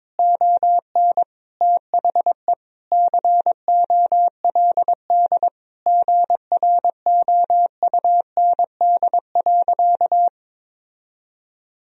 Morse code, heard as a keyed tone, about 22 wpm, 700 Hz. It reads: ON THE COLD GROUND.
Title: Sentences from Top 1000 Words - Review Part 3 - 22wpm